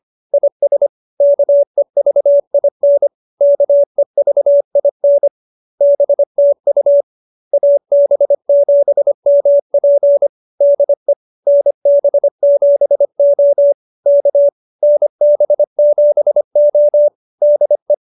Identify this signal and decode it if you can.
Morse code, heard as a keyed tone, about 25 wpm, 570 Hz. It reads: IS KEVIN KEVIN BTU AB7MP DE NB7O K NB7O DE